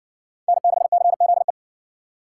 Morse code, heard as a keyed tone, about 57 wpm, 700 Hz. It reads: N6CZE